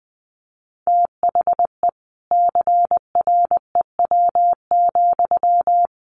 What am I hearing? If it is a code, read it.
Morse code, heard as a keyed tone, about 20 wpm, 700 Hz. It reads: THE CREW,
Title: Sentences from Top 1500 Words 20wpm